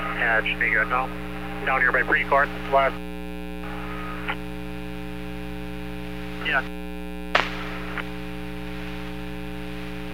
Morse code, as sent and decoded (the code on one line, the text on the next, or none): none